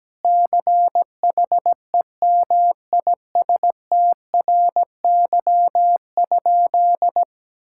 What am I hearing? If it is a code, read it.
Morse code, heard as a keyed tone, about 17 wpm, 700 Hz. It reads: CHEMISTRY?
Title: Sentences from Top 1000 Words - Review Part 6 - 17wpm